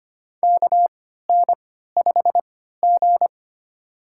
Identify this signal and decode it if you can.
Morse code, heard as a keyed tone, about 25 wpm, spaced as for 14 wpm, 700 Hz. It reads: KN5G
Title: USA Call Signs - 4 Characters - 3x Character Spacing - 25wpm